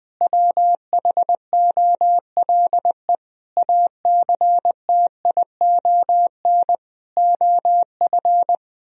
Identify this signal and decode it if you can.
Morse code, heard as a keyed tone, about 20 wpm, 700 Hz. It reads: WHOLE ACTION OF
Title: Sentences from Top 1000 Words - Review Part 4 - 20wpm